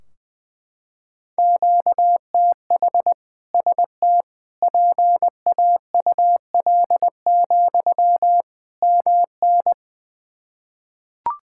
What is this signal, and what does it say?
Morse code, heard as a keyed tone, about 20 wpm, 700 Hz. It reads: QTH ST PAUL, MN
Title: K1USN SST QSO Practice 20wpm